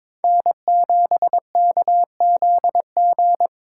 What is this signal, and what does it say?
Morse code, heard as a keyed tone, about 25 wpm, 700 Hz. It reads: N7KZG
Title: Speed Racing - USA Call Signs 22wpm